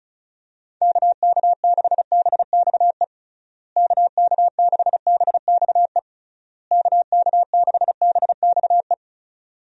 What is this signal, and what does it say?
Morse code, heard as a keyed tone, about 34 wpm, 700 Hz. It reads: KK6BXE KK6BXE KK6BXE
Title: USA Call Signs - 6 Characters - Repeated 3x - 35wpm